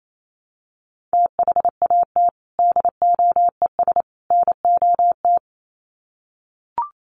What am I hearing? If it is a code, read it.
Morse code, heard as a keyed tone, about 28 wpm, 700 Hz. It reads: THAT DOES NOT
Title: Sets of 3 Words 28wpm